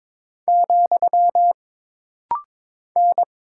Morse code, heard as a keyed tone, about 22 wpm, 700 Hz. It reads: , N
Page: This rapid-fire timing is geared to push your ICR (Instant Character Recognition) skills.